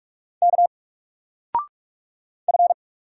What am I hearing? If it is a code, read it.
Morse code, heard as a keyed tone, about 44 wpm, 700 Hz. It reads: K F
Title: Single Letter-Number Alt 45wpm